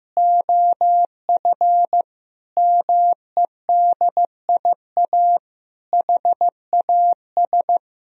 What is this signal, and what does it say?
Morse code, heard as a keyed tone, about 15 wpm, 700 Hz. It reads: OF MEDIA HAS
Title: Sentences from Top 1000 Words - Review Part 4 - 15wpm